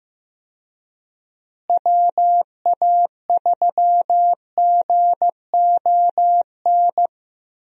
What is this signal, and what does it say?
Morse code, heard as a keyed tone, about 15 wpm, 700 Hz. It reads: WA3GON
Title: USA Call Signs - 6 Characters - Repeated 2x - 15wpm